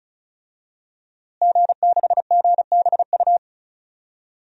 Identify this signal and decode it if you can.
Morse code, heard as a keyed tone, about 36 wpm, 700 Hz. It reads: G6GBU
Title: DXCC-100 Call Signs - 5 Characters - Repeated 3x - 35wpm